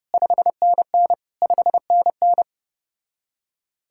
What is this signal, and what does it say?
Morse code, heard as a keyed tone, about 33 wpm, 700 Hz. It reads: 5NN 5NN